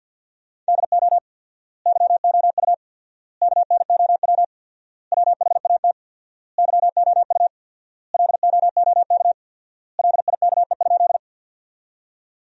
Morse code, heard as a keyed tone, about 50 wpm, 700 Hz. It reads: DO YOU KNOW WHAT YOU LOOK LIKE?
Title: 4x Word-Spacing - Sentences from Top 100 Words - 50wpm